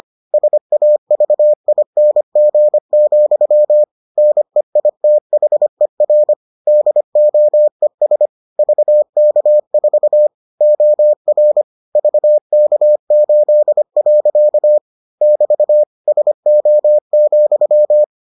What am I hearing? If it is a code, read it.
Morse code, heard as a keyed tone, about 25 wpm, 600 Hz. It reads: SAVING, NEITHER DOES VK4 OR VK8. <BT> SO,